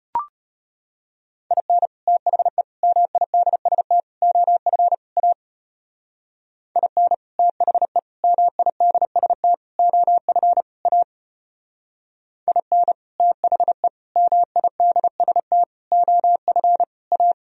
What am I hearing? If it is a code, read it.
Morse code, heard as a keyed tone, about 31 wpm, 700 Hz. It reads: IN THE MIDST OF A IN THE MIDST OF A IN THE MIDST OF A
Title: Speed-Racing - Sets of 5 Words 25wpm